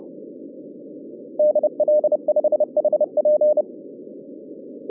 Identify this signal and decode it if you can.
Morse code, heard as a keyed tone, about 32 wpm, 590 Hz. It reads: DL5HP
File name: 2011 bcc-challenge samstag tape